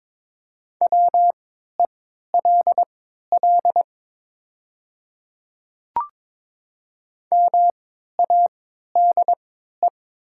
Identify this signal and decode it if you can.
Morse code, heard as a keyed tone, about 22 wpm, spaced as for 13 wpm, 700 Hz. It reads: WELL MADE
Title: 3x Character Spacing - Top 200 Words - 22wpm